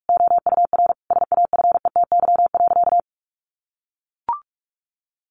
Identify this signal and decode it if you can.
Morse code, heard as a keyed tone, about 44 wpm, 700 Hz. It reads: OUR SAFETY.
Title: Sentences from Top 700 Words 45wpm